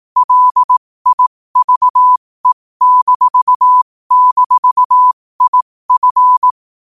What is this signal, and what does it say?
Morse code, heard as a keyed tone, about 18 wpm, spaced as for 15 wpm, 1000 Hz. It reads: LIVE--IF
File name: qotdfiles / 15wpm